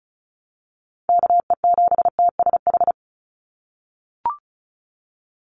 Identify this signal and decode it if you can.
Morse code, heard as a keyed tone, about 34 wpm, 700 Hz. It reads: KE7TSH
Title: USA Call Signs 35wpm